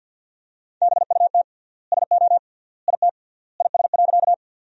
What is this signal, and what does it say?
Morse code, heard as a keyed tone, about 50 wpm, 700 Hz. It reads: BUT SO IT IS.
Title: 3x Word-Spacing - Sentences from Top 100 Words - 50wpm